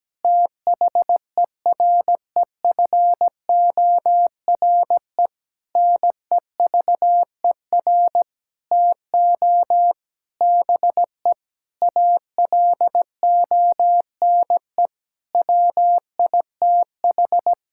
Morse code, heard as a keyed tone, about 17 wpm, 700 Hz. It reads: THEREFORE NEVER TO BE ALONE WITH
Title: Sentences from Top 1000 Words - Review Part 5 - 17wpm